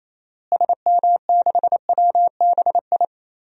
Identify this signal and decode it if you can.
Morse code, heard as a keyed tone, about 28 wpm, 700 Hz. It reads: SM6WBI
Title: DXCC-100 Call Signs - 6 Characters - Repeated 2x - 28wpm